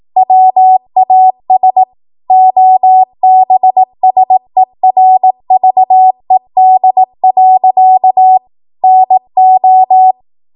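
Morse code, heard as a keyed tone, about 18 wpm, 750 Hz. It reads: WAS OBSERVED. NO
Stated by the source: 18 WPM Morse code practice from a national amateur radio association's code archive.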